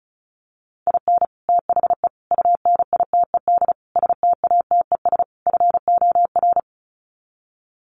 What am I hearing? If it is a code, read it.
Morse code, heard as a keyed tone, about 34 wpm, 700 Hz. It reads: IN THE UNITED STATES FOR
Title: Sets of 5 Words Encore2 35wpm